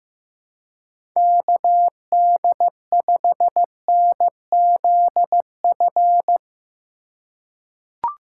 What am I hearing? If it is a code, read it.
Morse code, heard as a keyed tone, about 15 wpm, 700 Hz. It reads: KD5NZF
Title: USA Call Signs - 2x Character Spacing - 15wpm